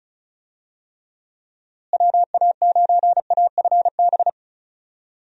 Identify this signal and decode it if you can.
Morse code, heard as a keyed tone, about 34 wpm, 700 Hz. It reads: WA9AFB
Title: USA Call Signs - 6 Characters - Repeated 3x - 35wpm